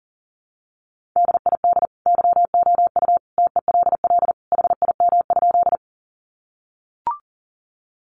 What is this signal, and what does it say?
Morse code, heard as a keyed tone, about 40 wpm, 700 Hz. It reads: DID YOU TELL HIM?
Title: Sentences from Top 200 Words 40wpm